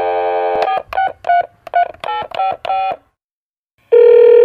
DTMF, heard as telephone touch tones, tones of about 110 ms. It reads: 468